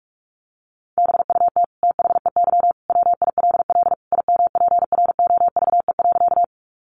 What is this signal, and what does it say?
Morse code, heard as a keyed tone, about 44 wpm, 700 Hz. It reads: BUT THEY WILL IMPROVE.
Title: Sentences from Top 800 Words 45wpm